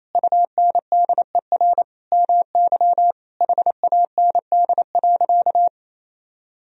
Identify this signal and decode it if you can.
Morse code, heard as a keyed tone, about 28 wpm, 700 Hz. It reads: UNDER MY HAND.
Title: Sentences from Top 1000 Words - Review Part 1 - 28wpm